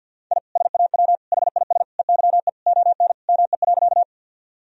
Morse code, heard as a keyed tone, about 51 wpm, 700 Hz. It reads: I SAW HIS EYE ON ME.
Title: Sentences from Top 1000 Words - Review Part 1 - 50wpm